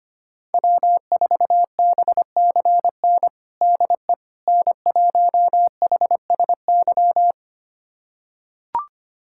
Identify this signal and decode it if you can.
Morse code, heard as a keyed tone, about 25 wpm, 700 Hz. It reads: W4BCN DE N1HSY
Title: USA Callsign DE Callsign - 5 Characters - 25wpm